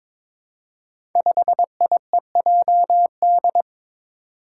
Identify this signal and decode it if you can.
Morse code, heard as a keyed tone, about 22 wpm, 700 Hz. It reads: 5IEJD
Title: Five Challenging Letter-Number Combinations 22wpm